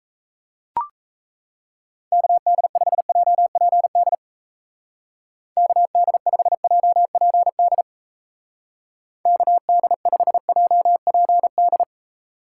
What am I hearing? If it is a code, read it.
Morse code, heard as a keyed tone, about 35 wpm, 700 Hz. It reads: KD5JPD KD5JPD KD5JPD
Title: Speed Racing - USA 2x3 Call Signs 28wpm